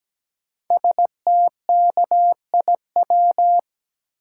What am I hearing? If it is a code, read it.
Morse code, heard as a keyed tone, about 17 wpm, 700 Hz. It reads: STKIW
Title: Code Groups - 5 Letters - 17wpm